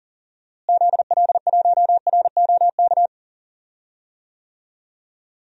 Morse code, heard as a keyed tone, about 40 wpm, 700 Hz. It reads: ZL1ROK
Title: DXCC-100 Call Signs - 6 Characters - Repeated 3x - 40wpm